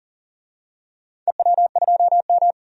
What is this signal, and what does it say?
Morse code, heard as a keyed tone, about 40 wpm, 700 Hz. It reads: EW2M
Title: DXCC-100 Call Signs - 4 Characters - 40wpm